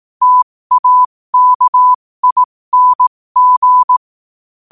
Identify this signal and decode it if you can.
Morse code, heard as a keyed tone, about 18 wpm, spaced as for 15 wpm, 1000 Hz. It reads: TAKING
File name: 15wpm